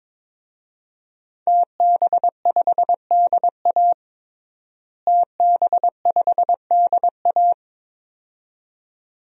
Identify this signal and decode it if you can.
Morse code, heard as a keyed tone, about 22 wpm, 700 Hz. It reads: TB5DA TB5DA
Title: DXCC-100 Call Signs - 5 Characters - Repeated 2x - 22wpm